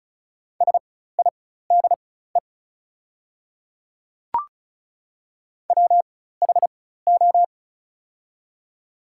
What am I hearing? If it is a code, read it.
Morse code, heard as a keyed tone, about 36 wpm, spaced as for 16 wpm, 690 Hz. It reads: SIDE WHO
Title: 4x Character Spacing - Top 100 Words - 35wpm